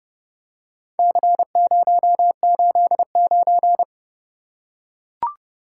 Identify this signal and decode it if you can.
Morse code, heard as a keyed tone, about 28 wpm, 700 Hz. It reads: C089